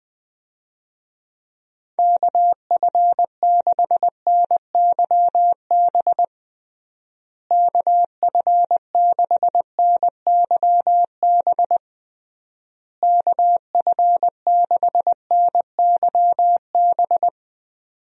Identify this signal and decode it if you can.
Morse code, heard as a keyed tone, about 21 wpm, 700 Hz. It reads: KF6NYB KF6NYB KF6NYB
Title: USA Call Signs - 6 Characters - Repeated 3x - 20wpm